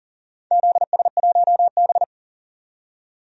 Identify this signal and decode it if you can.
Morse code, heard as a keyed tone, about 40 wpm, 700 Hz. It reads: ZS1B
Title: DXCC-100 Call Signs - 4 Characters - 40wpm